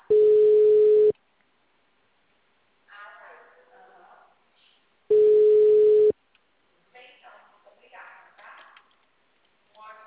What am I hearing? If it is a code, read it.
Ringback tone (Europe, 425 Hz), with speech.
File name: out- -2044- - - .14197